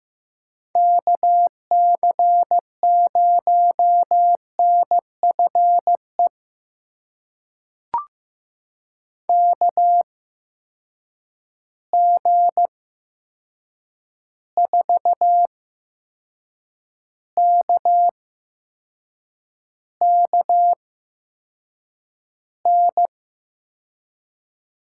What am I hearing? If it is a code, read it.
Morse code, heard as a keyed tone, about 15 wpm, spaced as for 6 wpm, 700 Hz. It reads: KC0NFE KG4KKN
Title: USA Call Signs - 6 Characters - 8x Character Spacing - 15wpm